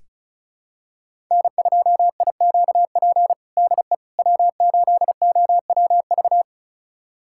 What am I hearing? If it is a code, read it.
Morse code, heard as a keyed tone, about 36 wpm, 700 Hz. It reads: N2IQP DE W8OWV
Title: USA Callsign DE Callsign - 5 Characters - 35wpm